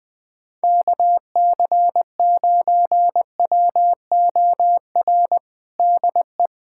Morse code, heard as a keyed tone, about 20 wpm, 700 Hz. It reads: KC9WOR DE
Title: USA Callsign DE Callsign 20wpm